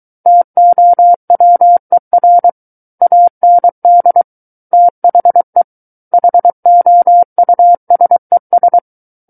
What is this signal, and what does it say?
Morse code, heard as a keyed tone, about 23 wpm, 700 Hz. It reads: TOWER AND THE HOUSES
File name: Book One / 14-Chapter 14 - 23 WPM